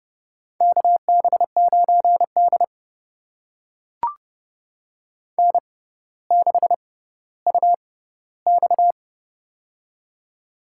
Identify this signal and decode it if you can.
Morse code, heard as a keyed tone, about 31 wpm, spaced as for 16 wpm, 700 Hz. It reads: KB9D N6UX
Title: USA Call Signs - 4 Characters - 6x Character Spacing - 30wpm